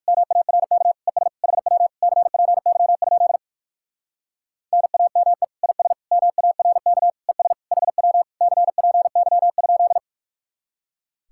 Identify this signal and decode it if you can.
Morse code, heard as a keyed tone, about 59 wpm, 700 Hz. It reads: MARK ES HW CPY? NAME IS MARK ES HW CPY?